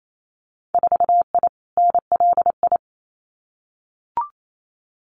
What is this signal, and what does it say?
Morse code, heard as a keyed tone, about 28 wpm, 700 Hz. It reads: 4I NLI
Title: Winter Field Day Exchanges Encore 28wpm